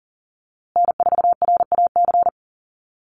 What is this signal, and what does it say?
Morse code, heard as a keyed tone, about 40 wpm, 700 Hz. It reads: N4RAC
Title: USA Call Signs Repeated 2x 40wpm